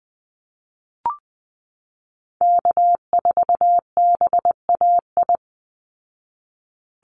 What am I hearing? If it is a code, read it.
Morse code, heard as a keyed tone, about 20 wpm, 700 Hz. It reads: K4BAI
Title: Call Signs Repeated 3x - 20wpm